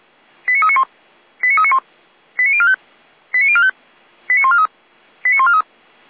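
Five-tone selective call (ZVEI), heard as quick runs of five tones, tones of about 70 ms. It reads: 89391 89391 89045 89045 89143 89143
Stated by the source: ZVEI five-tone call 89050 received on SDR.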